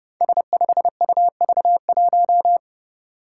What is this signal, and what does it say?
Morse code, heard as a keyed tone, about 30 wpm, 700 Hz. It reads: S5UV1
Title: Five Challenging Letter-Number Combinations 30wpm